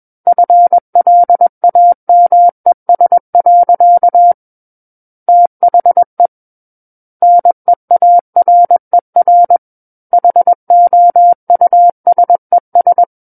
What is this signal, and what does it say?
Morse code, heard as a keyed tone, about 21 wpm, 700 Hz. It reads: FLAMES. THE NEARER HOUSES
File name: Book One / 12-Chapter 12 - 21 WPM